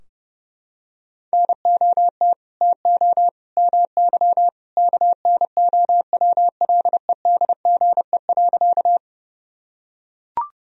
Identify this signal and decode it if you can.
Morse code, heard as a keyed tone, about 32 wpm, 700 Hz. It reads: NOT TO MY KNOWLEDGE.
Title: Sentences from Top 1000 Words - Review Part 3 - 30wpm